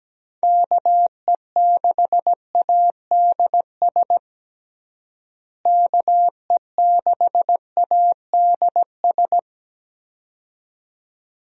Morse code, heard as a keyed tone, about 17 wpm, 700 Hz. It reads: KE6ADS KE6ADS
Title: USA Call Signs - 6 Characters - Repeated 2x - 17wpm